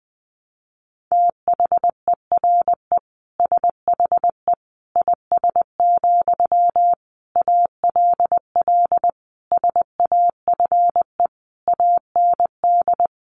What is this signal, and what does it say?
Morse code, heard as a keyed tone, about 20 wpm, 700 Hz. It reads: THERE SHE IS, ALL SAFE AND
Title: Sentences from Top 900 Words 20wpm